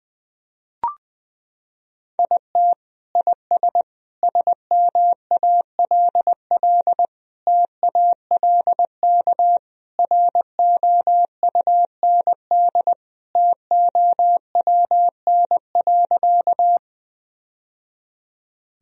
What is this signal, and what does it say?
Morse code, heard as a keyed tone, about 19 wpm, 700 Hz. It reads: IT IS SMALL TALK ROUND TOWN.
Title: Sentences from Top 1000 Words - Review Part 1 - 20wpm